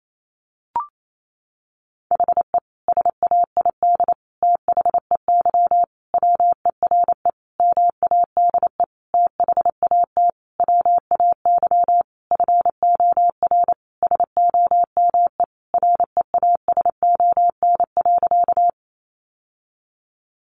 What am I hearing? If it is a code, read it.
Morse code, heard as a keyed tone, about 28 wpm, 700 Hz. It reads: HE SAID THEY WERE MADE THAT WAY FOR SOME REASON.
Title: Sentences from Top 700 Words 28wpm